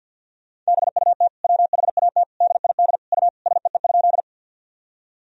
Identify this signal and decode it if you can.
Morse code, heard as a keyed tone, about 50 wpm, 700 Hz. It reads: BUT WHAT DID U SEE?
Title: Sentences from Top 100 Words and Abbreviations 50wpm